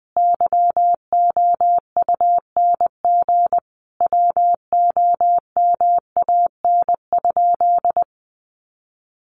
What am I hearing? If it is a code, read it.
Morse code, heard as a keyed tone, about 19 wpm, 700 Hz. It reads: YOUNG WOMAN?
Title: Sentences from Top 1100 Words 20wpm